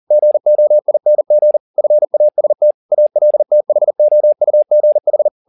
Morse code, heard as a keyed tone, about 40 wpm, 600 Hz. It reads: GOING FAST ALTHOUGH
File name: the old man and the sea 2 40wpm